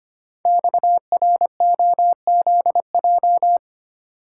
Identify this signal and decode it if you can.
Morse code, heard as a keyed tone, about 25 wpm, 700 Hz. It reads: XROZJ
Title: Code Groups - 5 Letters - 25wpm